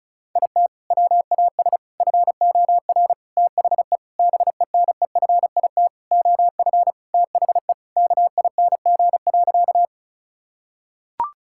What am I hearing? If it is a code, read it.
Morse code, heard as a keyed tone, about 34 wpm, 700 Hz. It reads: IT WAS FOR THE BENEFIT OF THE KING.
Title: Sentences from Top 1000 Words - Review Part 6 - 35wpm